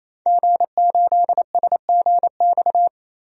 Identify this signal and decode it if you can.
Morse code, heard as a keyed tone, about 28 wpm, 700 Hz. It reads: G8SGX
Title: DXCC-100 Call Signs - 5 Characters - Repeated 3x - 28wpm